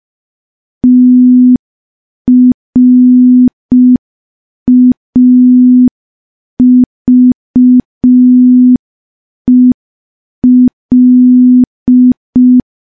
Morse code, heard as a keyed tone, about 5 wpm, 250 Hz. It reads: TRAVEL
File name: morse / morse-training-05WPM